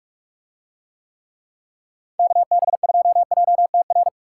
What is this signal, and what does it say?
Morse code, heard as a keyed tone, about 44 wpm, 700 Hz. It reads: KB2JTR